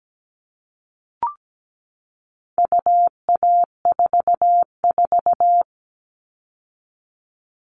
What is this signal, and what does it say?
Morse code, heard as a keyed tone, about 17 wpm, 700 Hz. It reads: UA44
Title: All Cut Numbers - 4-Digit 17wpm